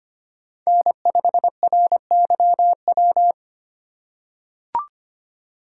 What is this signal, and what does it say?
Morse code, heard as a keyed tone, about 25 wpm, 700 Hz. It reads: N5RYW
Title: USA Call Signs - 6x Character Spacing - 25wpm